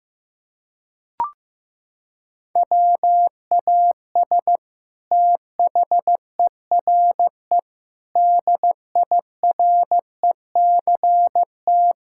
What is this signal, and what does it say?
Morse code, heard as a keyed tone, about 16 wpm, 700 Hz. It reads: WAS THERE DIRECT
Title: Sentences from Top 1000 Words - Review Part 6 - 15wpm